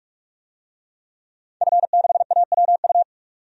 Morse code, heard as a keyed tone, about 46 wpm, 700 Hz. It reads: F6AWU
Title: DXCC-100 Call Signs - 5 Characters - 45wpm